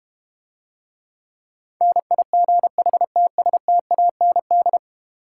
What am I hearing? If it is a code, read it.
Morse code, heard as a keyed tone, about 32 wpm, 700 Hz. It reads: NIGHTSTAND